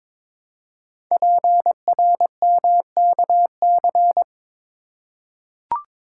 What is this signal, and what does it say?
Morse code, heard as a keyed tone, about 22 wpm, 700 Hz. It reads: PRMKC
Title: Code Groups - 5 Letters - 22wpm